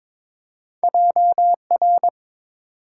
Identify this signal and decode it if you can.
Morse code, heard as a keyed tone, about 22 wpm, 700 Hz. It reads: JR
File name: practice / 22wpm / Two Letter-Number Combinations - 22wpm